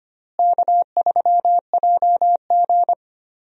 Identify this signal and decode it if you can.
Morse code, heard as a keyed tone, about 25 wpm, 700 Hz. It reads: K3JG
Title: USA Call Signs - 4 Characters - Repeated 3x - 25wpm